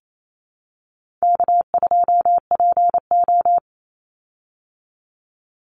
Morse code, heard as a keyed tone, about 28 wpm, 700 Hz. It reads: K2PO
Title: Easy Call Signs Repeated 2x - 28wpm